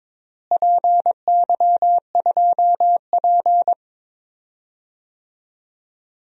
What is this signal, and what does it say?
Morse code, heard as a keyed tone, about 22 wpm, 700 Hz. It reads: PY2P
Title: DXCC-100 Call Signs - 4 Characters - Repeated 3x - 22wpm